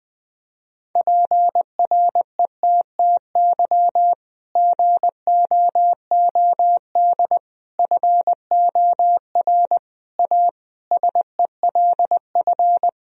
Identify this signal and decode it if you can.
Morse code, heard as a keyed tone, about 21 wpm, 700 Hz. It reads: PRETTY GOOD FOR A SELF
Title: Sentences from Top 1000 Words - Review Part 1 - 20wpm